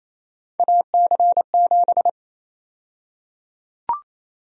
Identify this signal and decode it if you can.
Morse code, heard as a keyed tone, about 28 wpm, 700 Hz. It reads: AC7
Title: USA Call Sign Prefixes - Repeated 2x - 28wpm